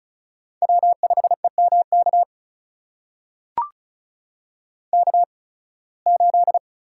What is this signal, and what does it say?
Morse code, heard as a keyed tone, about 36 wpm, spaced as for 23 wpm, 700 Hz. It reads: W5EMK K8
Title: USA Call Signs - 5 Characters - 8x Character Spacing - 35wpm